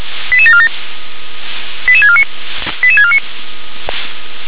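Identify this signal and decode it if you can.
Five-tone selective call (ZVEI), heard as quick runs of five tones, tones of about 70 ms. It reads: 88747 88649 88649
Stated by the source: ZVEI five-tone selective call received on an SDR.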